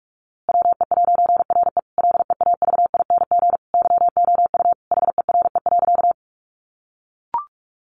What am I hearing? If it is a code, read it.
Morse code, heard as a keyed tone, about 44 wpm, 700 Hz. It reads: WE'RE LEAVING YOU HERE.
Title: A Sound Of Thunder 45wpm